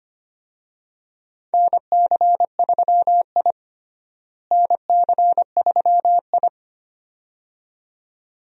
Morse code, heard as a keyed tone, about 25 wpm, 700 Hz. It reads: NC3I NC3I
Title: USA Call Signs - 4 Characters - Repeated 2x - 25wpm